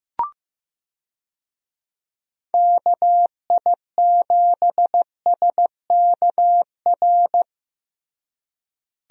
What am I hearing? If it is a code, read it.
Morse code, heard as a keyed tone, about 15 wpm, 700 Hz. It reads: KI7SKR
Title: USA Call Signs - 6 Characters - Repeated 2x - 15wpm